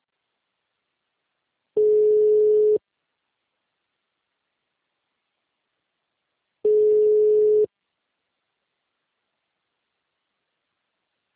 Ringback tone (Europe, 425 Hz).
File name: out- -2008